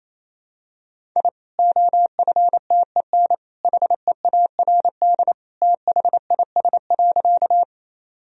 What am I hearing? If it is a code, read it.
Morse code, heard as a keyed tone, about 28 wpm, 700 Hz. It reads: I OFTEN HEARD THIS.
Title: Sentences from Top 1000 Words - Review Part 1 - 28wpm